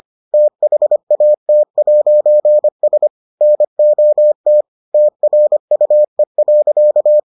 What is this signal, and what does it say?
Morse code, heard as a keyed tone, about 25 wpm, 600 Hz. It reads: THAT'S NOT TRUE.